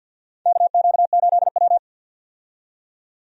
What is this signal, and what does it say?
Morse code, heard as a keyed tone, about 46 wpm, 700 Hz. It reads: KQ8W
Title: USA Call Signs - 4 Characters - 50wpm